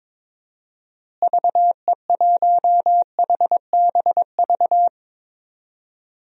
Morse code, heard as a keyed tone, about 21 wpm, 700 Hz. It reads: VE1HBV
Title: DXCC-100 Call Signs - 6 Characters - Repeated 3x - 22wpm